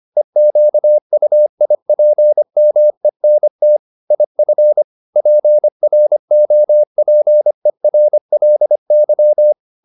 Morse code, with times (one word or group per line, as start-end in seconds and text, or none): 0.2-3.8 EQUIPMENT
4.1-4.8 IF
5.2-9.5 PROPERLY